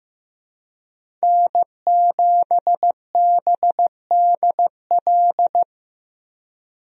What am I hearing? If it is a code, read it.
Morse code, heard as a keyed tone, about 15 wpm, 700 Hz. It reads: N7BDL
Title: USA Call Signs - 5 Characters - Repeated 3x - 15wpm